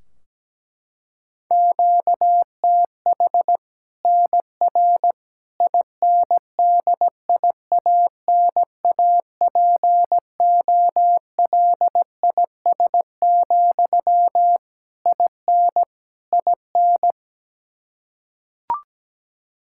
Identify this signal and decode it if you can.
Morse code, heard as a keyed tone, about 17 wpm, 700 Hz. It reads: QTH NR INDIANAPOLIS, IN IN
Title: Practice QSO - Exchange 1 - Beginning - 17wpm